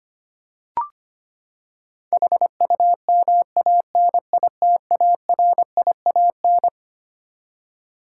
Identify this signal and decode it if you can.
Morse code, heard as a keyed tone, about 25 wpm, 700 Hz. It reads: HUMANITARIAN
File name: Twelve Letter Words - 25wpm